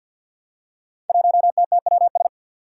Morse code, heard as a keyed tone, about 46 wpm, 700 Hz. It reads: 1TTWS
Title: Five Challenging Letter-Number Combinations 50wpm